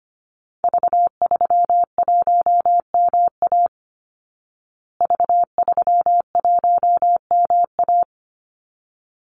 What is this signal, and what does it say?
Morse code, heard as a keyed tone, about 25 wpm, 700 Hz. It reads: V31MA V31MA
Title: Call Signs Repeated 3x - 25wpm